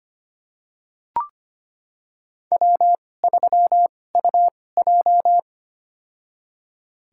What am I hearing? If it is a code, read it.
Morse code, heard as a keyed tone, about 25 wpm, spaced as for 18 wpm, 700 Hz. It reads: W3UJ
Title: USA Call Signs - 4 Characters - 2x Character Spacing - 25wpm